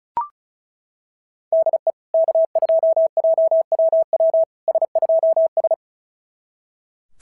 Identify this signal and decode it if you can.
Morse code, heard as a keyed tone, about 33 wpm, 650 Hz. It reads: DE K2JWW S2S
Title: SOTA QSO Practice 35wpm